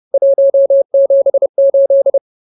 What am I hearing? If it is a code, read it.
Morse code, heard as a keyed tone, about 30 wpm, 550 Hz. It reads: 178
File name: CWT-Chapter-04